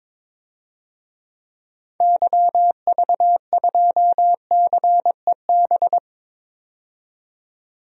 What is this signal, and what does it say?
Morse code, heard as a keyed tone, about 22 wpm, 700 Hz. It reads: YV2CEB
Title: DXCC-100 Call Signs - 6 Characters - 22wpm